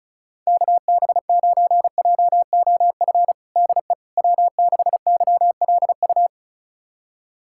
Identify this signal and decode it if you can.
Morse code, heard as a keyed tone, about 36 wpm, 700 Hz. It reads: KB9JOF DE W6YLU
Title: USA Callsign DE Callsign 35wpm